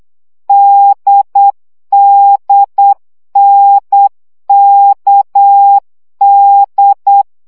Morse code, heard as a keyed tone, about 8 wpm, 800 Hz. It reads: DDNKD
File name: CW gyakorlas n d k 7 baud